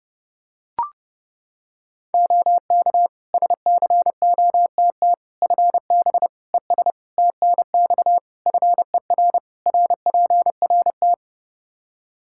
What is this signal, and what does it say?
Morse code, heard as a keyed tone, about 32 wpm, 700 Hz. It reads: OK SCOTT FB ES TNX FER RPRT
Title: Practice QSO - Exchange 2 - Info - 30wpm